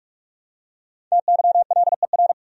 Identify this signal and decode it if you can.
Morse code, heard as a keyed tone, about 42 wpm, 700 Hz. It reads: TYLER
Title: USA Common Names - 45wpm